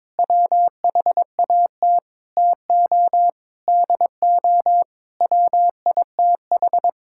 Morse code, heard as a keyed tone, about 22 wpm, 700 Hz. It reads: WHAT TO DO WITH
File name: practice / 22wpm / Build A Phrase - 5-Word Phrases - 22wpm